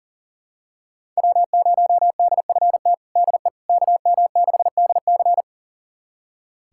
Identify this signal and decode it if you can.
Morse code, heard as a keyed tone, about 40 wpm, 700 Hz. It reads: W0DFT DE KM6DC
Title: USA Callsign DE Callsign - 5 Characters - 40wpm